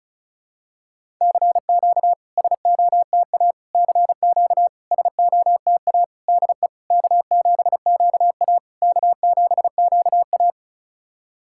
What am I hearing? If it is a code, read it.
Morse code, heard as a keyed tone, about 36 wpm, 700 Hz. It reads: CQ SOTA CQ SOTA DE K7QA K7QA
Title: SOTA QSO Practice 35wpm